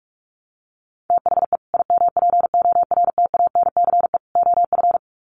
Morse code, heard as a keyed tone, about 44 wpm, 700 Hz. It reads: THE IMPORTANCE OF
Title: Sets of 3 Words 45wpm